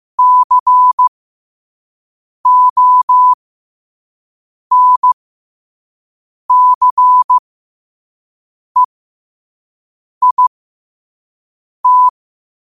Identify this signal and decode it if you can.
Morse code, heard as a keyed tone, about 15 wpm, spaced as for 5 wpm, 1000 Hz. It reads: CONCEIT
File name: qotdfiles / 5wpm